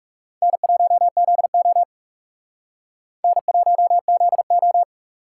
Morse code, heard as a keyed tone, about 39 wpm, 700 Hz. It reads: N1ZO N1ZO